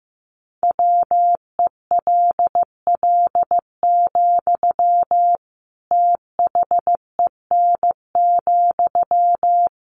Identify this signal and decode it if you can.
Morse code, heard as a keyed tone, about 15 wpm, 700 Hz. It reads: WELL, THEN,
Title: Sentences from Top 700 Words 15wpm